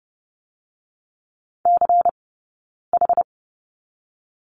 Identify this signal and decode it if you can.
Morse code, heard as a keyed tone, about 30 wpm, 700 Hz. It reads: C H
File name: 092 Lesson 16d - Letter c - Two Characters 30wpm